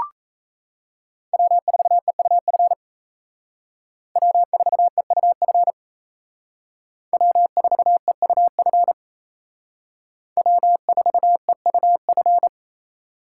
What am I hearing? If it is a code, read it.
Morse code, heard as a keyed tone, about 35 wpm, 700 Hz. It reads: W4EUF W4EUF W4EUF W4EUF